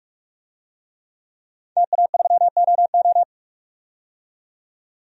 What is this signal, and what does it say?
Morse code, heard as a keyed tone, about 46 wpm, 700 Hz. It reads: TA3OO
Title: DXCC-100 Call Signs - 5 Characters - 45wpm